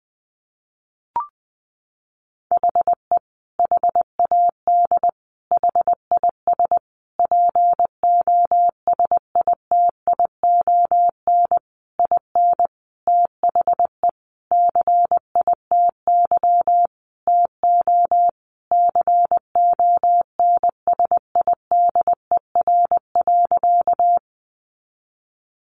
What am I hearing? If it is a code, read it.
Morse code, heard as a keyed tone, about 20 wpm, 700 Hz. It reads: HE HAD HIS POSITION IN THE CITY TO CONSIDER.
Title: Sentences from Top 900 Words 20wpm